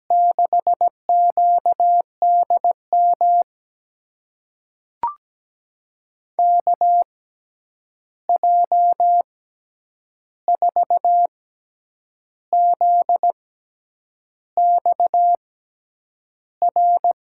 Morse code, heard as a keyed tone, about 17 wpm, spaced as for 8 wpm, 700 Hz. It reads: 6QDM KJ4ZXR
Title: USA Call Signs - 6 Characters - 6x Character Spacing - 17wpm